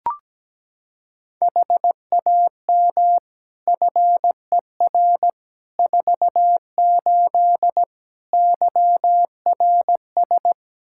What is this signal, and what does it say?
Morse code, heard as a keyed tone, about 17 wpm, 700 Hz. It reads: HAM FER 48 YRS
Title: Practice QSO - Exchange 3 - More Info - 17wpm